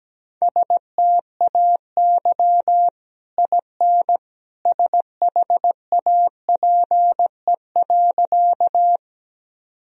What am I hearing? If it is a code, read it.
Morse code, heard as a keyed tone, about 17 wpm, 700 Hz. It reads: STAY IN SHAPE.
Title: Sentences from Top 1000 Words - Review Part 3 - 17wpm